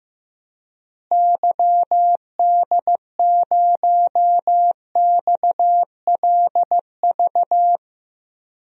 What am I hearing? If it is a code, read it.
Morse code, heard as a keyed tone, about 15 wpm, 700 Hz. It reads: YD0XLV
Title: DXCC-100 Call Signs - 6 Characters - Repeated 3x - 15wpm